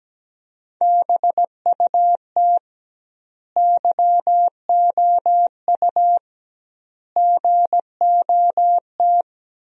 Morse code, heard as a keyed tone, about 17 wpm, 700 Hz. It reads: BUT YOU GOT
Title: 2x Word-Spacing - Sentences from Top 300 Words - 17wpm